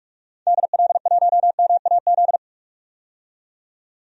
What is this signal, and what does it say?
Morse code, heard as a keyed tone, about 46 wpm, 700 Hz. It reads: DL1MAZ
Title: DXCC-100 Call Signs - 6 Characters - 45wpm